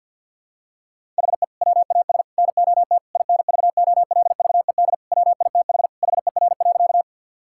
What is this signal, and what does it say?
Morse code, heard as a keyed tone, about 53 wpm, 690 Hz. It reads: HE WAS NOT INVOLVED WITH HER.
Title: Sentences from Top 1000 Words - Review Part 5 - 50wpm